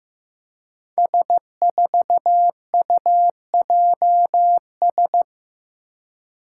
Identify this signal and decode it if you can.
Morse code, heard as a keyed tone, about 15 wpm, 700 Hz. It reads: S4UJS
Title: Five Challenging Letter-Number Combinations 15wpm